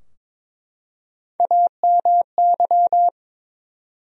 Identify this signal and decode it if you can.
Morse code, heard as a keyed tone, about 21 wpm, 700 Hz. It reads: AMY
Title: USA Common Names - Max 4 Letters - 22wpm